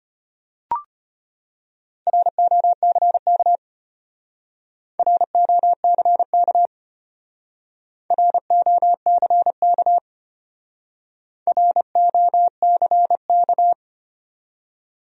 Morse code, heard as a keyed tone, about 34 wpm, 700 Hz. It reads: ROCK ROCK ROCK ROCK